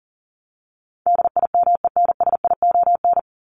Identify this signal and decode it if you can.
Morse code, heard as a keyed tone, about 40 wpm, 700 Hz. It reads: DIMENSION
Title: Top 1501-1750 Words 40wpm